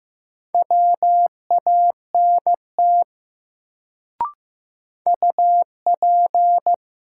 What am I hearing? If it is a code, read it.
Morse code, heard as a keyed tone, about 15 wpm, 700 Hz. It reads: WANT UP
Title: Sets of 2 Words - New Word Drill - 15wpm